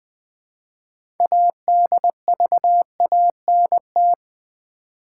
Morse code, heard as a keyed tone, about 20 wpm, 700 Hz. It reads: ADVANT
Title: Build A Word - 3 Letters At A Time - 9-Letter Words - 20wpm